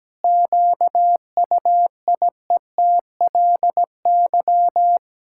Morse code, heard as a keyed tone, about 17 wpm, 700 Hz. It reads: QUIETLY
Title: Pangram Challenge v2 - 17wpm